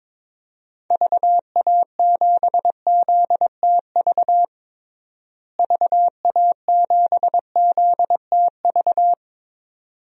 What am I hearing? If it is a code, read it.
Morse code, heard as a keyed tone, about 21 wpm, 700 Hz. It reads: VA7ZTV VA7ZTV
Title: DXCC-100 Call Signs - 6 Characters - Repeated 2x - 22wpm